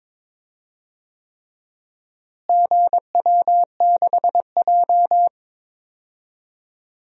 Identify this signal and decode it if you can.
Morse code, heard as a keyed tone, about 21 wpm, 700 Hz. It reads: GW6J
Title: DXCC-100 Call Signs - 4 Characters - Repeated 3x - 22wpm